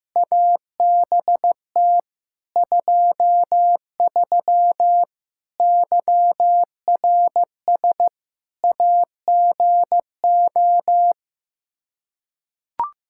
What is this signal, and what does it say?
Morse code, heard as a keyed tone, about 15 wpm, 700 Hz. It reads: ABT 23 YRS AGO
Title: Practice QSO - Exchange 3 - More Info - 15wpm